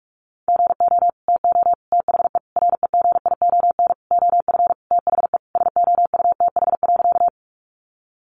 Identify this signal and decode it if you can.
Morse code, heard as a keyed tone, about 44 wpm, 700 Hz. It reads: GO TO THE REGION OF THE SOUTH.
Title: Sentences from Top 1500 Words 45wpm